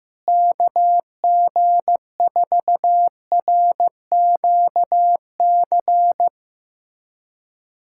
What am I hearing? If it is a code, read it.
Morse code, heard as a keyed tone, about 15 wpm, 700 Hz. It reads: KG4RQC
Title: USA Callsign DE Callsign 15wpm